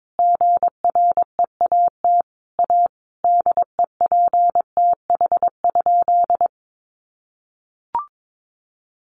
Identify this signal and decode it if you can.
Morse code, heard as a keyed tone, about 22 wpm, 700 Hz. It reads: GREAT A DEPTH?
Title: Sentences from Top 1200 Words 22wpm